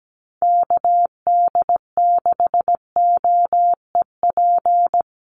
Morse code, heard as a keyed tone, about 17 wpm, 700 Hz. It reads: KD6OEP
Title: USA Call Signs Repeated 2x 17wpm